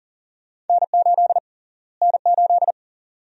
Morse code, heard as a keyed tone, about 40 wpm, 700 Hz. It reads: N8 N8
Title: USA Call Sign Prefixes - Repeated 2x - 40wpm